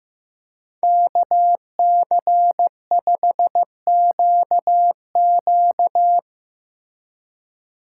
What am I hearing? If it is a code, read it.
Morse code, heard as a keyed tone, about 17 wpm, 700 Hz. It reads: KC5QQ
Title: Speed Racing - USA 2x2 Call Signs 15wpm